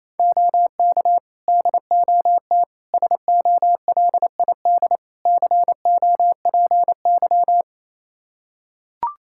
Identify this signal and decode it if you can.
Morse code, heard as a keyed tone, about 27 wpm, 700 Hz. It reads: OK DOT SOLID COPY
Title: Practice QSO - Exchange 3 - More Info - 28wpm